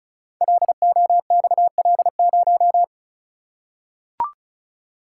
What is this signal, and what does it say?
Morse code, heard as a keyed tone, about 36 wpm, 700 Hz. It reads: LOXL0
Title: Code Groups - 5 Letter:Numbers - 35wpm